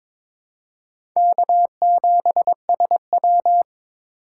Morse code, heard as a keyed tone, about 22 wpm, 700 Hz. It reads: K7SW
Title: USA Call Signs - 4x Character Spacing - 22wpm